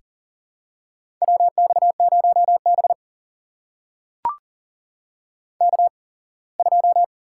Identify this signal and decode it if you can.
Morse code, heard as a keyed tone, about 40 wpm, spaced as for 24 wpm, 700 Hz. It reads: WX0B K2
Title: USA Call Signs - 4 Characters - 8x Character Spacing - 40wpm